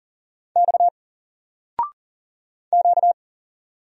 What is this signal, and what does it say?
Morse code, heard as a keyed tone, about 40 wpm, 700 Hz. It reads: X Q
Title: Single Letter-Number 40wpm